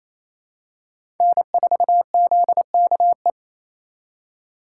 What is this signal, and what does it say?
Morse code, heard as a keyed tone, about 28 wpm, 700 Hz. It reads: N4ZKE
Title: USA Call Signs - 5 Characters - 4x Character Spacing - 28wpm